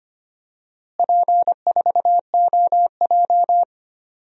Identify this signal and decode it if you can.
Morse code, heard as a keyed tone, about 25 wpm, 700 Hz. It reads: P4OJ
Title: DXCC-100 Call Signs - 4 Characters - Repeated 3x - 25wpm